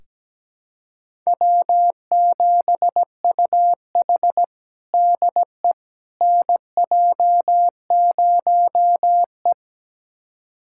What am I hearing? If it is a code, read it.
Morse code, heard as a keyed tone, about 17 wpm, 700 Hz. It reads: W7UH DE NJ0E
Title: USA Callsign DE Callsign - 4 Characters - 17wpm